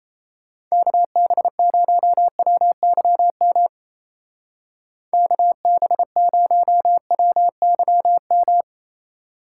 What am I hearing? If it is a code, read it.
Morse code, heard as a keyed tone, about 35 wpm, 700 Hz. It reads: KB0WYM KB0WYM
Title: Speed Racing - USA 2x3 Call Signs 28wpm